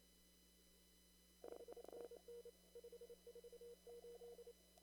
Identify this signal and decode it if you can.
Morse code, heard as a keyed tone, about 30 wpm, 480 Hz. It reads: 5NN 548
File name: CW / kt5j wpx cw-op-k5pi